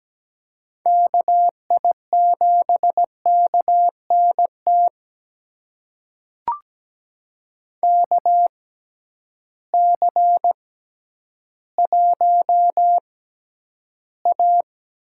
Morse code, heard as a keyed tone, about 17 wpm, spaced as for 10 wpm, 700 Hz. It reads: KI7KNT KC1A
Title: USA Call Signs - 6 Characters - 6x Character Spacing - 17wpm